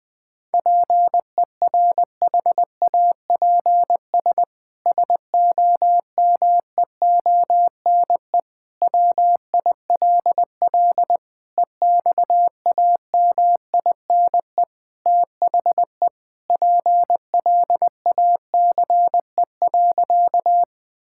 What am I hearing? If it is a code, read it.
Morse code, heard as a keyed tone, about 20 wpm, 700 Hz. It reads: PERHAPS SOMEONE WILL EXAMINE THE PLACE.
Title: Sentences from Top 1600 Words 20wpm